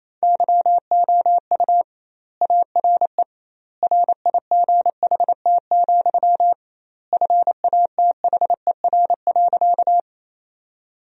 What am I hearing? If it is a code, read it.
Morse code, heard as a keyed tone, about 27 wpm, 700 Hz. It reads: YOU ARE RIGHT, FATHER.
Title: 2x Word-Spacing - Sentences from Top 300 Words - 28wpm